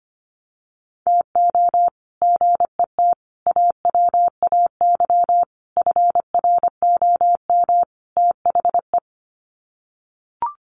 Morse code, heard as a keyed tone, about 25 wpm, 700 Hz. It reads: TO GET AWAY FROM THE
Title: Sets of 5 Words Encore2 25wpm